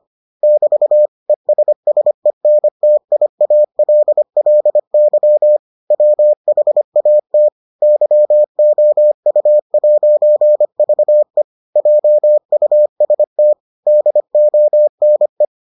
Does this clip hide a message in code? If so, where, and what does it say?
Yes: Morse code from 0.4 to 15.4 s, heard as a keyed tone, about 25 wpm, 600 Hz. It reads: <BT> ESSENTIALLY WHAT YOU'VE JUST DONE